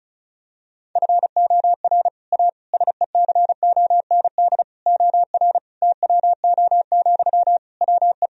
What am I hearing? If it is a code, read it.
Morse code, heard as a keyed tone, about 36 wpm, 700 Hz. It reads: FOR A SECOND OR TWO, WE
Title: Sentences from Top 1000 Words - Review Part 1 - 35wpm